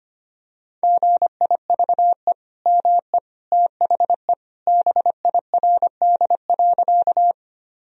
Morse code, heard as a keyed tone, about 25 wpm, 700 Hz. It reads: GIVE ME THE BIRD.
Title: Sentences from Top 1000 Words - Review Part 2 - 25wpm